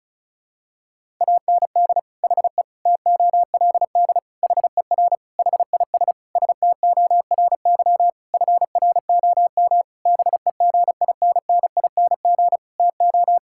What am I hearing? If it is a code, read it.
Morse code, heard as a keyed tone, about 34 wpm, 700 Hz. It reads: AND HE TOLD HER HIS STORY FROM BEGINNING TO
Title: Sentences from Top 1000 Words - Review Part 5 - 35wpm